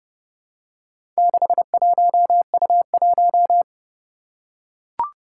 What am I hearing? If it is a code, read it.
Morse code, heard as a keyed tone, about 28 wpm, 700 Hz. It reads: 61U1
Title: Four Challenging Letter-Number Combinations 30wpm